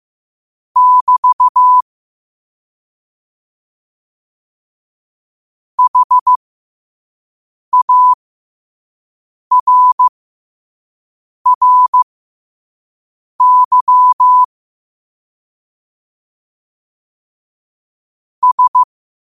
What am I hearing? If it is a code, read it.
Morse code, heard as a keyed tone, about 15 wpm, spaced as for 5 wpm, 1000 Hz. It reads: <BT> HARRY S
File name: qotdfiles / 5wpm